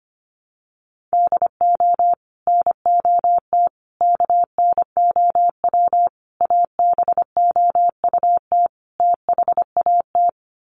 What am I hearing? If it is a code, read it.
Morse code, heard as a keyed tone, about 25 wpm, 700 Hz. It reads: DO NOT KNOW ABOUT THAT
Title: Sets of 5 Words Encore2 25wpm